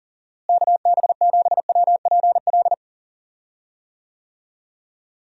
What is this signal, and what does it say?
Morse code, heard as a keyed tone, about 40 wpm, 700 Hz. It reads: KB7WPL
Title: USA Call Signs - 6 Characters - Repeated 2x - 40wpm